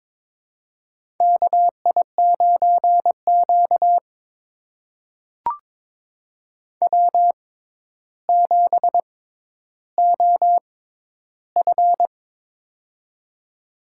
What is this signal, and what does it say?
Morse code, heard as a keyed tone, about 22 wpm, spaced as for 11 wpm, 700 Hz. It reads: KI9Q W7OF
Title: USA Call Signs - 4 Characters - 6x Character Spacing - 22wpm